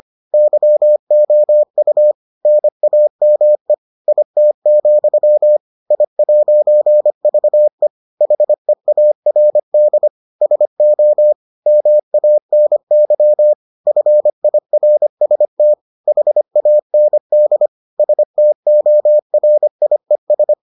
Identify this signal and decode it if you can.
Morse code, heard as a keyed tone, about 25 wpm, 600 Hz. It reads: YOU NAME IT, I'VE HEARD SO MANY FIRST HAND STORIES